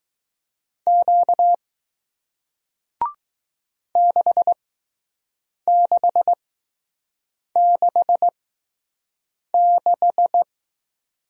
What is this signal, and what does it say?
Morse code, heard as a keyed tone, about 20 wpm, spaced as for 6 wpm, 700 Hz. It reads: Q 6666